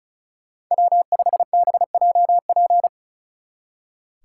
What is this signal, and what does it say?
Morse code, heard as a keyed tone, about 43 wpm, 700 Hz. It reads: W5BJP
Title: Speed Racing - USA 1x3 Call Signs 35wpm